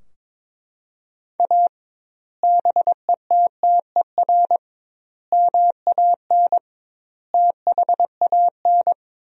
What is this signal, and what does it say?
Morse code, heard as a keyed tone, about 22 wpm, 700 Hz. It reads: A BETTER MAN THAN
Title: 2x Word-Spacing - Sentences from Top 400 Words - 22wpm